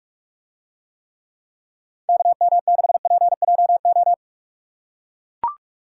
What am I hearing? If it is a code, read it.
Morse code, heard as a keyed tone, about 44 wpm, 700 Hz. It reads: KM6PJO